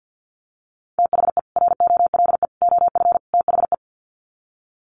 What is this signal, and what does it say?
Morse code, heard as a keyed tone, about 50 wpm, 700 Hz. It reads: THE ROLE OF THE
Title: Sets of 4 Words 50wpm